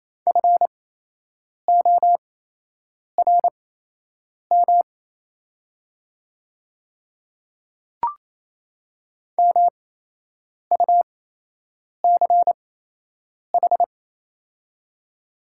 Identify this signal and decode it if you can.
Morse code, heard as a keyed tone, about 28 wpm, spaced as for 8 wpm, 700 Hz. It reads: FORM MUCH
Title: 8x Character Spacing - Top 300 Words - 28wpm